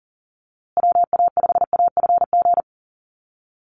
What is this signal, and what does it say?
Morse code, heard as a keyed tone, about 40 wpm, 700 Hz. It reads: WA5AFG
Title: USA Call Signs Repeated 2x 40wpm